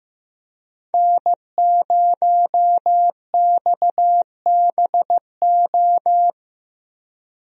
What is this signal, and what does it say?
Morse code, heard as a keyed tone, about 15 wpm, 700 Hz. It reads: N0XBO
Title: USA Call Signs - 5 Characters - Repeated 2x - 15wpm